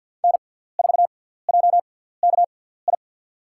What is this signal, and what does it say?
Morse code, heard as a keyed tone, about 47 wpm, spaced as for 17 wpm, 690 Hz. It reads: N4JKI
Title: USA Call Signs - 6x Character Spacing - 50wpm